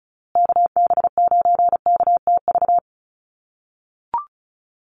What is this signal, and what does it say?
Morse code, heard as a keyed tone, about 34 wpm, 700 Hz. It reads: KB9KTV
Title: USA Call Signs Repeated 2x 35wpm